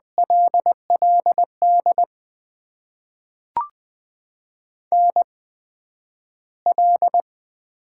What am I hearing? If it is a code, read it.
Morse code, heard as a keyed tone, about 20 wpm, spaced as for 11 wpm, 700 Hz. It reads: LLD NL
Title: USA Call Signs - 8x Character Spacing - 20wpm